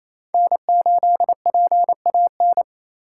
Morse code, heard as a keyed tone, about 28 wpm, 700 Hz. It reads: N8PAN
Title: USA Call Signs - 3x Character Spacing - 28wpm